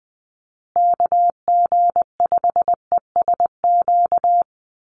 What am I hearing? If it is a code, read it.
Morse code, heard as a keyed tone, about 20 wpm, 700 Hz. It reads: KG5ESQ
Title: USA Call Signs Repeated 3x 20wpm